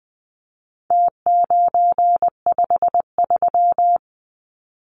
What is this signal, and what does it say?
Morse code, heard as a keyed tone, about 20 wpm, 700 Hz. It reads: T953
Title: Numbers - Contest Style - 4-Digit 20wpm